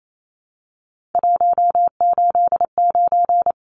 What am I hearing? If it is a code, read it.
Morse code, heard as a keyed tone, about 28 wpm, 700 Hz. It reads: 189
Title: Numbers - 3-Digit 28wpm